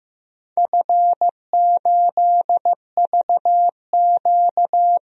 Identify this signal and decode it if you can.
Morse code, heard as a keyed tone, about 15 wpm, 700 Hz. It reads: F8VQ
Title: DXCC-100 Call Signs - 4 Characters - 15wpm